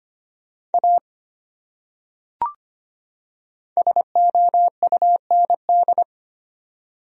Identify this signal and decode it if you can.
Morse code, heard as a keyed tone, about 25 wpm, 700 Hz. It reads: A SOUND
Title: Top 100 Words 25wpm